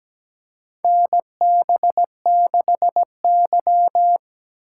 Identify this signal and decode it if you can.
Morse code, heard as a keyed tone, about 17 wpm, 700 Hz. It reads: NB6Y
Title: USA Call Signs - 4 Characters - 4x Character Spacing - 17wpm